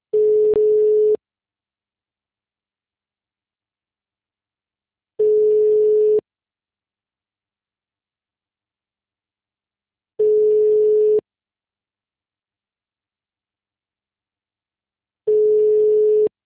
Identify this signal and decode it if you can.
Ringback tone (Europe, 425 Hz).
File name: out- -2042- - - .81408